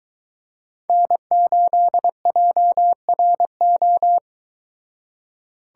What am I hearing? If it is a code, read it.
Morse code, heard as a keyed tone, about 22 wpm, 700 Hz. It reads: N8JRO